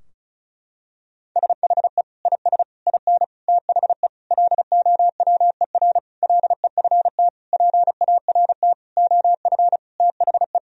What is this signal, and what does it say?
Morse code, heard as a keyed tone, about 36 wpm, 700 Hz. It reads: SHE IS IN THE LOWER LEFT PART OF THE
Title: Sentences from Top 1000 Words - Review Part 4 - 35wpm